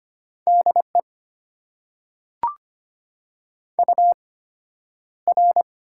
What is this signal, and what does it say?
Morse code, heard as a keyed tone, about 25 wpm, spaced as for 11 wpm, 690 Hz. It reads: DE UR
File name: Top 90 QSO Elements - 8x Character Spacing - 25wpm